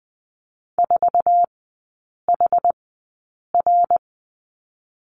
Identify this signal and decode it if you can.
Morse code, heard as a keyed tone, about 20 wpm, spaced as for 8 wpm, 700 Hz. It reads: 4HR
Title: 087 Lesson 15e - Number 5 - Three Characters 20wpm